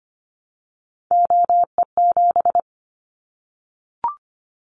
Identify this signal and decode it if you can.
Morse code, heard as a keyed tone, about 25 wpm, 700 Hz. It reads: OE7
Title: Callsign Prefixes for Top 100 DXCC - 25wpm